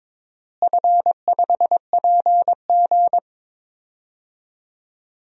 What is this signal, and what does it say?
Morse code, heard as a keyed tone, about 21 wpm, 700 Hz. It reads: F5PG
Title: DXCC-100 Call Signs - 4 Characters - Repeated 3x - 22wpm